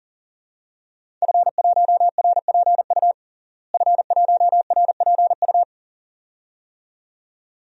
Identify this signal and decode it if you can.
Morse code, heard as a keyed tone, about 40 wpm, 700 Hz. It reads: F1RPU F1RPU
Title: DXCC-100 Call Signs - 5 Characters - Repeated 2x - 40wpm